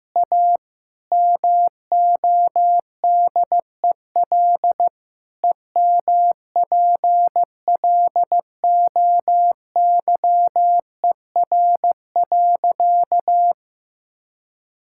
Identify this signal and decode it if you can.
Morse code, heard as a keyed tone, about 15 wpm, 700 Hz. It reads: A MODEL EMPLOYER.
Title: Sentences from Top 1600 Words 15wpm